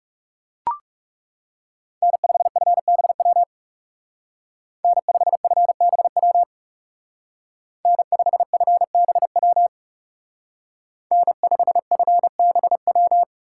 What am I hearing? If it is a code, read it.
Morse code, heard as a keyed tone, about 37 wpm, 700 Hz. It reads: N5FBW N5FBW N5FBW N5FBW
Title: Speed Racing - USA 1x3 Call Signs 30wpm